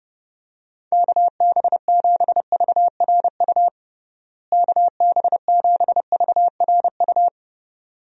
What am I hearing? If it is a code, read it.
Morse code, heard as a keyed tone, about 30 wpm, 700 Hz. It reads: KB7VRU KB7VRU
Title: USA Call Signs - 6 Characters - Repeated 2x - 30wpm